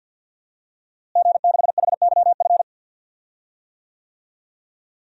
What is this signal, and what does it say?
Morse code, heard as a keyed tone, about 52 wpm, 700 Hz. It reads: G6HYF
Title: DXCC-100 Call Signs - 5 Characters - Repeated 3x - 50wpm